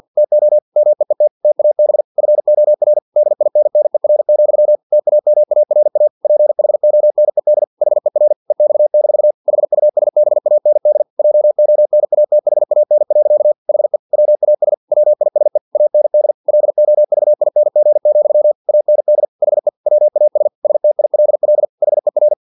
Morse code, heard as a keyed tone, about 50 wpm, 600 Hz. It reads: TO MEET TAB FOR DINNER, TAMARA PHONED HER EX- HUSBAND JONATHAN. HE WAS WISE AND LOVING, AND HE WAS STILL HER
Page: Never 49wpm